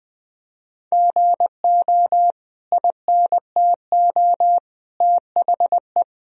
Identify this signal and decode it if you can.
Morse code, heard as a keyed tone, about 20 wpm, 700 Hz. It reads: GO INTO THE
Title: Sentences from Top 1000 Words - Review Part 4 - 20wpm